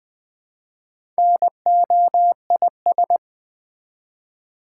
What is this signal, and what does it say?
Morse code, heard as a keyed tone, about 20 wpm, 700 Hz. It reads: NOIS
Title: Build A Word - 1 Letter At A Time - 3 to 5-Letter Words - 20wpm